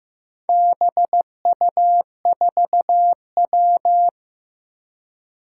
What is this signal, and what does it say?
Morse code, heard as a keyed tone, about 15 wpm, 700 Hz. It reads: BU4W
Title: Four Challenging Letter-Number Combinations 15wpm